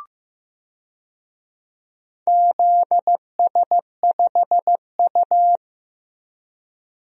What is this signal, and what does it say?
Morse code, heard as a keyed tone, about 15 wpm, 700 Hz. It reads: ZS5U
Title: DXCC-100 Call Signs - 4 Characters - Repeated 2x - 15wpm